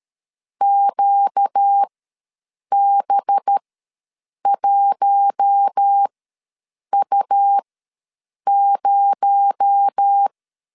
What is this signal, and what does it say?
Morse code, heard as a keyed tone, about 13 wpm, spaced as for 7 wpm, 800 Hz. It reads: QB1U0
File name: LEZIONI / 8 YJ45 / 8 Y ws 12wpm 3409